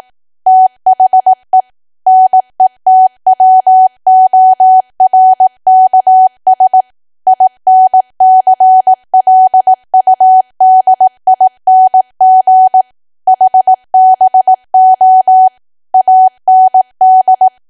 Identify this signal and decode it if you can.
Morse code, heard as a keyed tone, about 18 wpm, 750 Hz. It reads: THE NETWORKS INCLUDING HBO AND
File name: Archive / 18 WPM / 18WPM